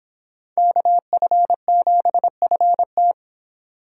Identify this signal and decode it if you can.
Morse code, heard as a keyed tone, about 26 wpm, 700 Hz. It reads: KF7FT